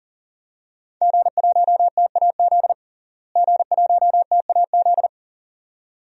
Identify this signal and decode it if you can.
Morse code, heard as a keyed tone, about 40 wpm, 700 Hz. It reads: G1TAZ G1TAZ
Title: DXCC-100 Call Signs - 5 Characters - Repeated 2x - 40wpm